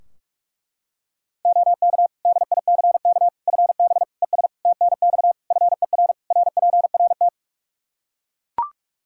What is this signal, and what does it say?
Morse code, heard as a keyed tone, about 41 wpm, 700 Hz. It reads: OK DICK FB ES TNX FER RPRT
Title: Practice QSO - Exchange 2 - Info - 45wpm